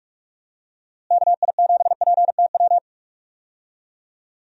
Morse code, heard as a keyed tone, about 46 wpm, 700 Hz. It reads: KI7PTW
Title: USA Call Signs - 6 Characters - Repeated 2x - 45wpm